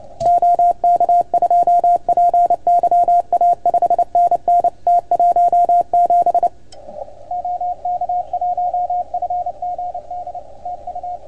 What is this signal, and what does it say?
Morse code, heard as a keyed tone, about 30 wpm, 690 Hz. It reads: OK2PYA 5NN T17 OK1FGD TU
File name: Archiv / CUC310 / OK1FGD CUC 310 pileup